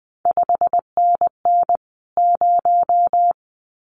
Morse code, heard as a keyed tone, about 20 wpm, 700 Hz. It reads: 5NN 0
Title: CQ WW DX Contest Exchange Encore - 20wpm